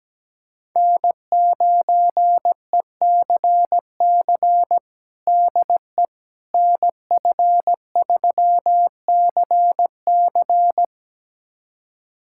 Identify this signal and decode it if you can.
Morse code, heard as a keyed tone, about 17 wpm, 700 Hz. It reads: N9ECC DE NF3CC
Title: USA Callsign DE Callsign 17wpm